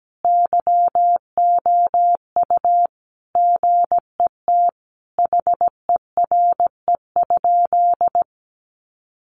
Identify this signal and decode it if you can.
Morse code, heard as a keyed tone, about 17 wpm, 700 Hz. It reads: YOU GET HERE?
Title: Sentences from Top 200 Words 17wpm